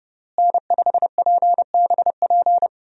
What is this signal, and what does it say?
Morse code, heard as a keyed tone, about 32 wpm, 700 Hz. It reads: N5PBP
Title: USA Call Signs - 5 Characters - Repeated 3x - 30wpm